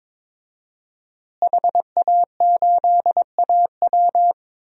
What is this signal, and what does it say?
Morse code, heard as a keyed tone, about 22 wpm, 700 Hz. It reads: HA8AW
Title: DXCC-100 Call Signs - 5 Characters - Repeated 3x - 22wpm